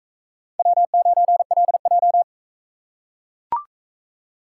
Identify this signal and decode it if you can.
Morse code, heard as a keyed tone, about 43 wpm, 700 Hz. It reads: W9LJ